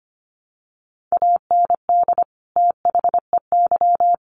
Morse code, heard as a keyed tone, about 25 wpm, 700 Hz. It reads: AND THEY
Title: Sets of 2 Words 25wpm